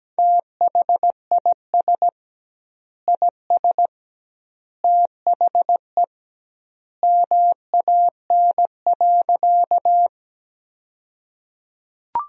Morse code, heard as a keyed tone, about 17 wpm, 700 Hz. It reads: THIS IS THE MAN.
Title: 2x Word-Spacing - Sentences from Top 200 Words - 17wpm